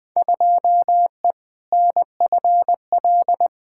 Morse code, heard as a keyed tone, about 20 wpm, 700 Hz. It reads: 2E NFL
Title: Field Day Exchanges Encore - 20wpm